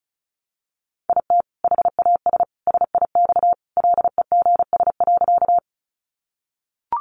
Morse code, heard as a keyed tone, about 34 wpm, 700 Hz. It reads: IT HAS SIX LEGS.
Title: Additional Easy Phrases 35wpm